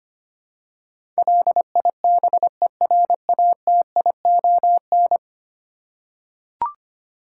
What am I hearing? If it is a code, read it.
Morse code, heard as a keyed tone, about 25 wpm, 700 Hz. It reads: LIBERATION
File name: 10 Letter Words - First 3 Letters Given - 25wpm